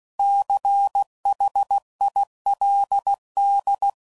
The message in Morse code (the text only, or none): CHILD